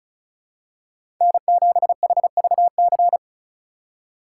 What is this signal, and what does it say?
Morse code, heard as a keyed tone, about 34 wpm, 700 Hz. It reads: N7HVC
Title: USA Call Signs - 4x Character Spacing - 35wpm